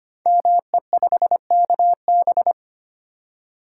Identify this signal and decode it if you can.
Morse code, heard as a keyed tone, about 25 wpm, 700 Hz. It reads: ME5KB
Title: Code Groups - 5 Letter:Numbers - 25wpm